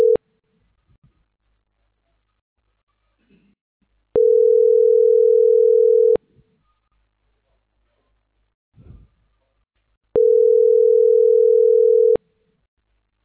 Ringback tone (North America), 440 + 480 Hz.